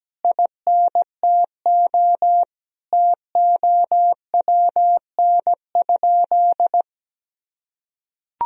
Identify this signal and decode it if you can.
Morse code, heard as a keyed tone, about 17 wpm, 700 Hz. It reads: INTO TOWN?
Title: Sentences from Top 1000 Words - Review Part 2 - 17wpm